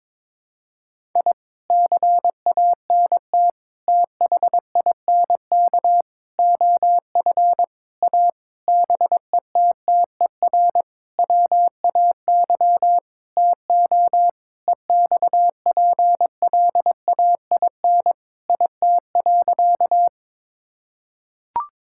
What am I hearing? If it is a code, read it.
Morse code, heard as a keyed tone, about 22 wpm, 700 Hz. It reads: I CANT THINK OF A BETTER WAY TO EXPLAIN IT.
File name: practice / 22wpm / Sentences with 8-Word Phrases - 22wpm